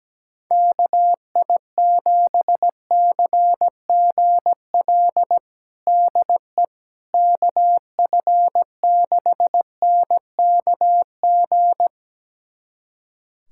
Morse code, heard as a keyed tone, about 17 wpm, 700 Hz. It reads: KI7CGL DE KF6NKG
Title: USA Callsign DE Callsign - 6 Characters - 17wpm